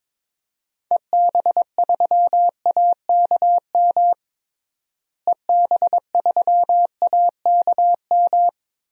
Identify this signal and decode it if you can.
Morse code, heard as a keyed tone, about 22 wpm, 700 Hz. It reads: EB3AKM EB3AKM
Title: DXCC-100 Call Signs - 6 Characters - Repeated 3x - 22wpm